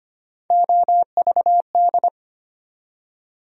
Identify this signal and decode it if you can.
Morse code, heard as a keyed tone, about 25 wpm, 700 Hz. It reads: OVD
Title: USA Call Sign Postfixes - 3 Letters 25wpm